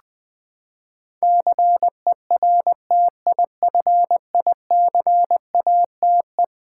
Morse code, heard as a keyed tone, about 20 wpm, 700 Hz. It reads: CERTIFICATE
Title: Eleven Letter Words - 20wpm